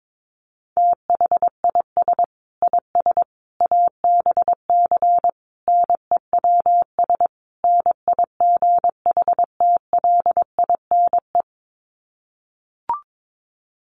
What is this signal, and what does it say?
Morse code, heard as a keyed tone, about 22 wpm, 700 Hz. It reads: THIS IS ABC NEWS NIGHTLINE
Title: Sets of 5 Words 22wpm